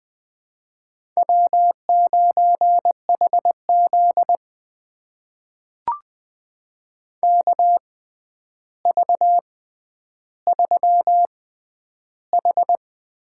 Morse code, heard as a keyed tone, about 20 wpm, spaced as for 10 wpm, 700 Hz. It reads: W9HZ KV3H
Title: USA Call Signs - 4 Characters - 6x Character Spacing - 20wpm